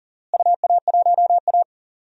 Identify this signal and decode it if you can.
Morse code, heard as a keyed tone, about 40 wpm, 700 Hz. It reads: UA1A
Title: DXCC-100 Call Signs - 4 Characters - Repeated 2x - 40wpm